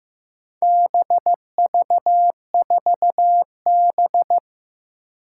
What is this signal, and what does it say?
Morse code, heard as a keyed tone, about 15 wpm, 700 Hz. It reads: BV4B
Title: Four Challenging Letter-Number Combinations 15wpm